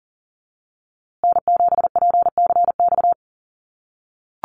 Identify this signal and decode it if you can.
Morse code, heard as a keyed tone, about 40 wpm, 700 Hz. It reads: N7PCX
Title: USA Call Signs Repeated 3x 40wpm